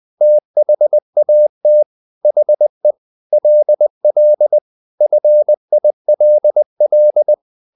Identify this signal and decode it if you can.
Morse code, heard as a keyed tone, about 20 wpm, 600 Hz. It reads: THAT HE LL FILL
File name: the old man and the sea / the old man and the sea part 1 20wpm